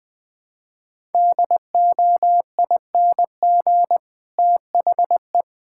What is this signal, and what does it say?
Morse code, heard as a keyed tone, about 20 wpm, 700 Hz. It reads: DOING THE
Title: Sentences with 7-Word Phrases - 20wpm